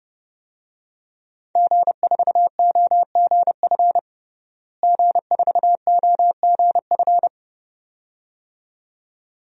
Morse code, heard as a keyed tone, about 30 wpm, 700 Hz. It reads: G4OGF G4OGF
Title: DXCC-100 Call Signs - 5 Characters - Repeated 2x - 30wpm